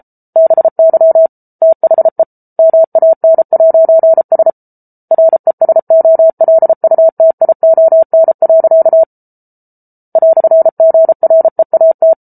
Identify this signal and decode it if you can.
Morse code, heard as a keyed tone, about 34 wpm, 650 Hz. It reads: BY THE MAN'S RESOLUTION. "GREAT